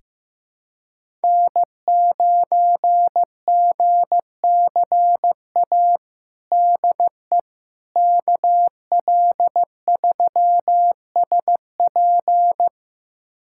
Morse code, heard as a keyed tone, about 16 wpm, 700 Hz. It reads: N9GCA DE KL3SP
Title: USA Callsign DE Callsign - 5 Characters - 15wpm